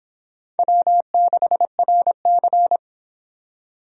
Morse code, heard as a keyed tone, about 26 wpm, 700 Hz. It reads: W6RC